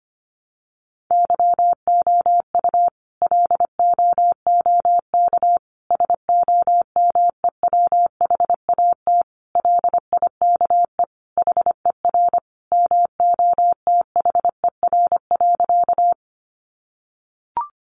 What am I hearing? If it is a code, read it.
Morse code, heard as a keyed tone, about 25 wpm, 700 Hz. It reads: YOU LOOK SOMEWHAT LIKE HER MOTHER.
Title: Sentences from Top 1300 Words 25wpm